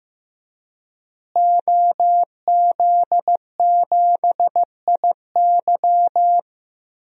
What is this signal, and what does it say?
Morse code, heard as a keyed tone, about 15 wpm, 700 Hz. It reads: OZ7IY
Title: DXCC-100 Call Signs - 5 Characters - Repeated 3x - 15wpm